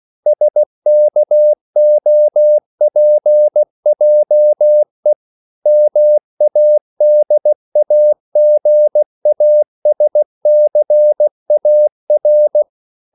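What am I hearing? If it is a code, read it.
Morse code, heard as a keyed tone, about 16 wpm, 600 Hz. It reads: SKOPJE MADAGASCAR
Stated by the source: CW practice of national capitals at 16 words per minute, from ON6ZQ.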